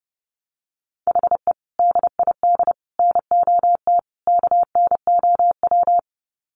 Morse code, heard as a keyed tone, about 28 wpm, 700 Hz. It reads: HE DID NOT KNOW
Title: Sets of 4 Words 30wpm